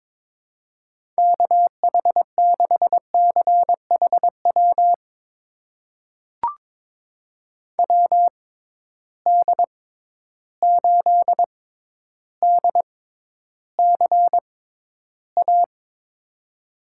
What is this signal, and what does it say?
Morse code, heard as a keyed tone, about 22 wpm, spaced as for 11 wpm, 700 Hz. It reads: KH6CHW WD8DCA
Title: USA Call Signs - 6 Characters - 6x Character Spacing - 22wpm